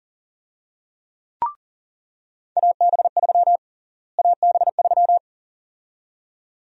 Morse code, heard as a keyed tone, about 41 wpm, 700 Hz. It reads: AB3 AB3
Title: USA Call Sign Prefixes - Repeated 2x - 40wpm